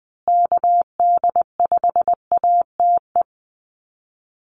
Morse code, heard as a keyed tone, about 20 wpm, 700 Hz. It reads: KD5ATE
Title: USA Call Signs Repeated 2x 20wpm